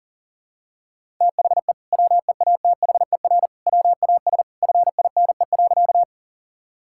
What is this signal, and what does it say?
Morse code, heard as a keyed tone, about 40 wpm, 700 Hz. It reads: THE WEATHER WAS FINE.
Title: Sentences from Top 1000 Words - Review Part 5 - 40wpm